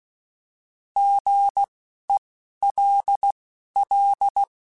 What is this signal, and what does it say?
Morse code, heard as a keyed tone, about 16 wpm, spaced as for 11 wpm, 800 Hz. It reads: GELL
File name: Alphabetical / Alphabetical mp3s / Gs @10